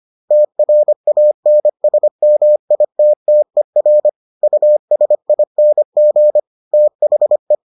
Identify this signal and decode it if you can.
Morse code, heard as a keyed tone, about 25 wpm, 600 Hz. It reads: TRANSMITTER USING THE